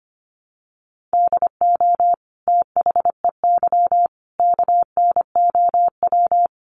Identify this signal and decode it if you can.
Morse code, heard as a keyed tone, about 25 wpm, 700 Hz. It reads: DO THEY KNOW
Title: Sentences from Top 100 Words 25wpm